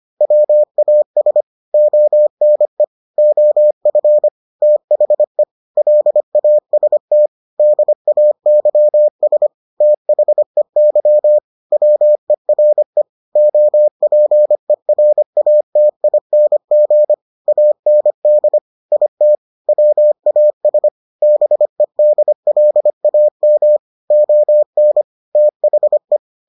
Morse code, heard as a keyed tone, about 25 wpm, 600 Hz. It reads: WAS ONE OF THE LAST DAYS THEY WERE OPERATING AND IT WAS BEDLAM ON THE